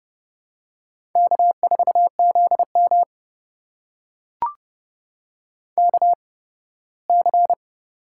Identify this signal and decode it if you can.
Morse code, heard as a keyed tone, about 30 wpm, spaced as for 18 wpm, 700 Hz. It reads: K4ZM KC
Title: USA Call Signs - 8x Character Spacing - 30wpm